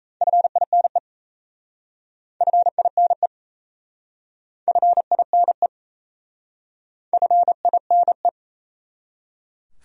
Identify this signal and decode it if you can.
Morse code, heard as a keyed tone, about 37 wpm, 700 Hz. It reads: FINE FINE FINE FINE